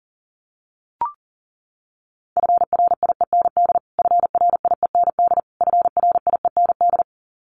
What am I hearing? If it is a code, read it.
Morse code, heard as a keyed tone, about 40 wpm, 700 Hz. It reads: FRIEND FRIEND FRIEND
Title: Top 500 Words Repeated 40wpm